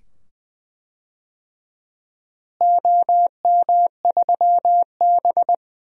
Morse code, heard as a keyed tone, about 20 wpm, 700 Hz. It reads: OM3B
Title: DXCC-100 Call Signs - 4 Characters - Repeated 2x - 20wpm